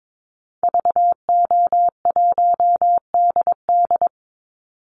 Morse code, heard as a keyed tone, about 22 wpm, 700 Hz. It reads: VO1DD
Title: Call Signs Repeated 3x - 22wpm